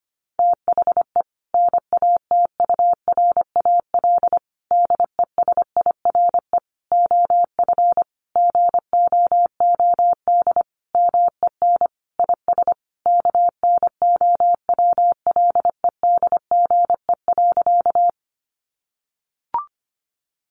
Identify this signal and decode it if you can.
Morse code, heard as a keyed tone, about 25 wpm, 700 Hz. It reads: THE NATURAL DESIRE OF GOOD MEN IS KNOWLEDGE.
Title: Sentences from Top 1100 Words 25wpm